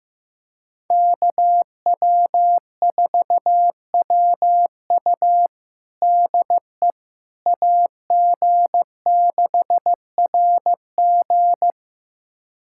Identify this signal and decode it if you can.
Morse code, heard as a keyed tone, about 15 wpm, 700 Hz. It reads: KW4WU DE AG6RG
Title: USA Callsign DE Callsign - 5 Characters - 15wpm